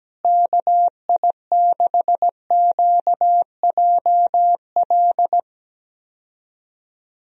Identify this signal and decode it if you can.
Morse code, heard as a keyed tone, about 17 wpm, 700 Hz. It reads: KI6QJL
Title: USA Call Signs - 6 Characters - Repeated 2x - 17wpm